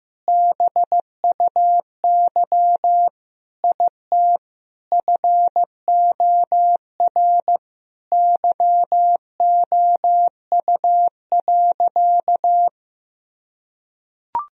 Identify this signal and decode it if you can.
Morse code, heard as a keyed tone, about 15 wpm, 700 Hz. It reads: BUY IT FOR YOU.
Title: Sentences from Top 1000 Words - Review Part 4 - 15wpm